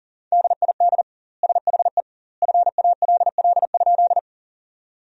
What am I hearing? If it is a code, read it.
Morse code, heard as a keyed tone, about 40 wpm, 700 Hz. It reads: DID SHE FALL?
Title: 2x Word-Spacing - Sentences from Top 500 Words - 40wpm